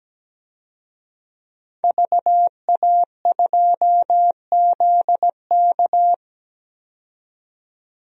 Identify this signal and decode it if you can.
Morse code, heard as a keyed tone, about 17 wpm, 700 Hz. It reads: VA2ZK
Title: DXCC-100 Call Signs - 5 Characters - 17wpm